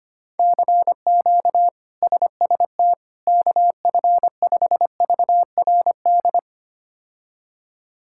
Morse code, heard as a keyed tone, about 25 wpm, 700 Hz. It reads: CQ SST KF5VRD
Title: K1USN SST QSO Practice - 25wpm